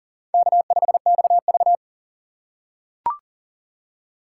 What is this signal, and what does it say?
Morse code, heard as a keyed tone, about 40 wpm, 700 Hz. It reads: K5XV
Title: USA Call Signs - 3x Character Spacing - 40wpm